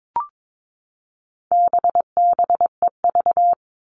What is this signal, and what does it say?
Morse code, heard as a keyed tone, about 22 wpm, 700 Hz. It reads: BBEV
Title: All Cut Numbers - 4-Digit 22wpm